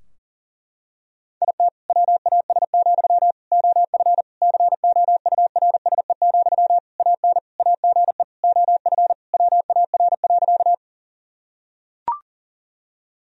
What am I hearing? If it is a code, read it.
Morse code, heard as a keyed tone, about 37 wpm, 700 Hz. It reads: IT WAS, OF COURSE, AN AGE OF WAR.
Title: Sentences from Top 1000 Words - Review Part 3 - 40wpm